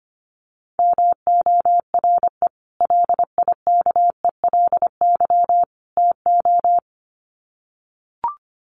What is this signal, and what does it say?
Morse code, heard as a keyed tone, about 24 wpm, 700 Hz. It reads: MORE LIKELY TO
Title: Sets of 3 Words 25wpm